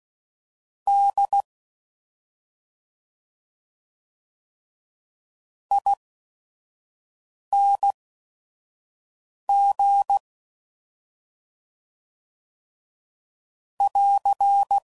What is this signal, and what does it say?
Morse code, heard as a keyed tone, about 16 wpm, spaced as for 5 wpm, 800 Hz. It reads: D ING <AR>